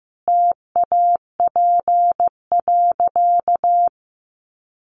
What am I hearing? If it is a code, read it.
Morse code, heard as a keyed tone, about 15 wpm, 700 Hz. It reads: TAP.
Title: Sentences from Top 1400 Words 15wpm